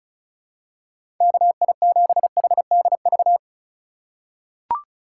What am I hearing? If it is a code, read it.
Morse code, heard as a keyed tone, about 36 wpm, 700 Hz. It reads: KI7HDV
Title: USA Call Signs - 2x Character Spacing - 35wpm